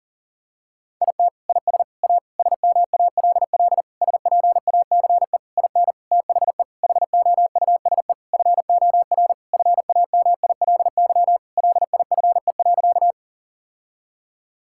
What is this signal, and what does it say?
Morse code, heard as a keyed tone, about 37 wpm, 700 Hz. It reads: IT IS A SMALL SPACE IN THE HOUSE FOR FAMILY LIFE.
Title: Sentences from Top 1000 Words - Review Part 5 - 40wpm